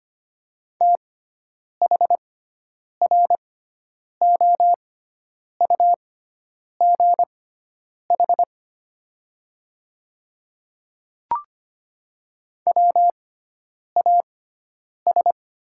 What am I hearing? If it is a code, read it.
Morse code, heard as a keyed tone, about 25 wpm, spaced as for 9 wpm, 690 Hz. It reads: THROUGH WAS
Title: 6x Character Spacing - Top 200 Words - 25wpm